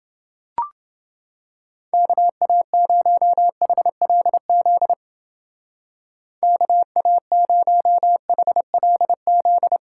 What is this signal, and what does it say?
Morse code, heard as a keyed tone, about 29 wpm, 700 Hz. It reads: KA0HLZ KA0HLZ